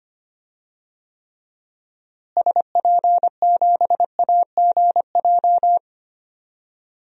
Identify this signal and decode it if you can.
Morse code, heard as a keyed tone, about 25 wpm, 700 Hz. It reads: SP7AGJ
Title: DXCC-100 Call Signs - 6 Characters - Repeated 2x - 25wpm